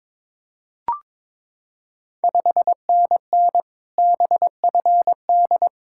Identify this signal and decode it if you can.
Morse code, heard as a keyed tone, about 21 wpm, 700 Hz. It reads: 5NN BFD
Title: Ontario QSO Party Exchange 22wpm